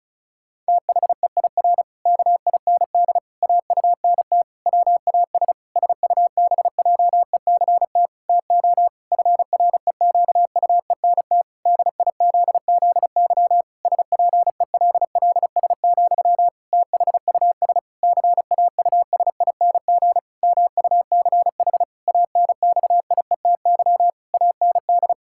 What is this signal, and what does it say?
Morse code, heard as a keyed tone, about 36 wpm, 700 Hz. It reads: THEIR KIND AUNT WAS SUBJECT TO FREQUENT DIZZY SPELLS, THUS CAUSING MUCH ANXIETY AND
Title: Pangram Challenge v2 - 35wpm